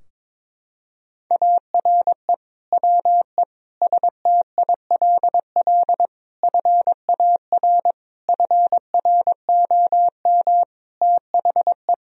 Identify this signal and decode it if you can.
Morse code, heard as a keyed tone, about 22 wpm, 700 Hz. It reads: ARE WE STILL FAR FROM THE
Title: Sentences from Top 1000 Words - Review Part 4 - 22wpm